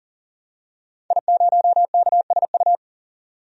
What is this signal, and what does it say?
Morse code, heard as a keyed tone, about 40 wpm, 700 Hz. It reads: I0KSU
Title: DXCC-100 Call Signs - 5 Characters - 40wpm